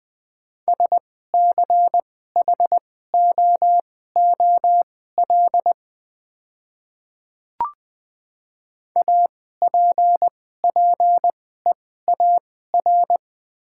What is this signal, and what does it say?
Morse code, heard as a keyed tone, about 20 wpm, spaced as for 14 wpm, 700 Hz. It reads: SCHOOL APPEAR
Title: 2x Character Spacing - Top 500 Words - 20wpm